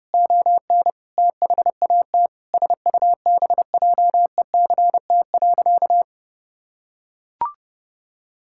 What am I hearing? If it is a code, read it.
Morse code, heard as a keyed tone, about 30 wpm, 700 Hz. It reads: ON THAT SUBJECT.
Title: Sentences from Top 1600 Words 30wpm